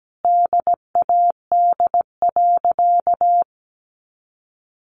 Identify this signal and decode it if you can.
Morse code, heard as a keyed tone, about 17 wpm, 700 Hz. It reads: DAD.
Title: Easy Phrases 17wpm